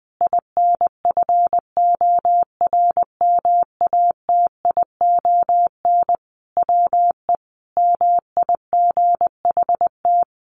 Morse code, heard as a keyed tone, about 20 wpm, 700 Hz. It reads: INFORMATION WE MIGHT
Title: Sentences from Top 1300 Words 20wpm